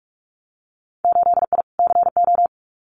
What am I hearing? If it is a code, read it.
Morse code, heard as a keyed tone, about 46 wpm, 700 Hz. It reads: 8I CO
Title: Winter Field Day Exchanges Encore 45wpm